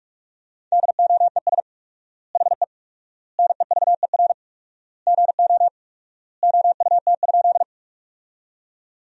Morse code, heard as a keyed tone, about 41 wpm, 700 Hz. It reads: DOES HE NEVER GO OUT?
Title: 4x Word-Spacing - Sentences from Top 300 Words - 45wpm